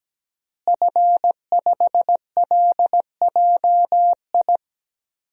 Morse code, heard as a keyed tone, about 17 wpm, 700 Hz. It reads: F5LJI
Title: DXCC-100 Call Signs - 5 Characters - Repeated 3x - 17wpm